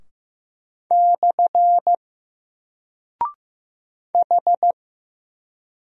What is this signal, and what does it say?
Morse code, heard as a keyed tone, about 15 wpm, 700 Hz. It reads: / H
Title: Single Character (Letter:Number:Punctuation:Prosign) 15wpm